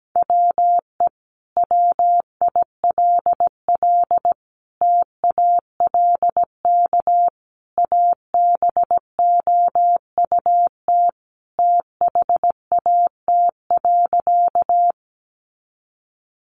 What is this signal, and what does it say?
Morse code, heard as a keyed tone, about 17 wpm, 700 Hz. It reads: WE WILL TALK ABOUT THAT.
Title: Sentences from Top 400 Words 17wpm